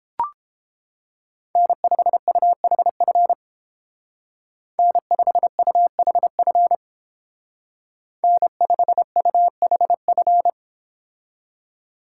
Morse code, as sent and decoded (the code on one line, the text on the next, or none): -. ..... ..- .... ..-. / -. ..... ..- .... ..-. / -. ..... ..- .... ..-.
N5UHF N5UHF N5UHF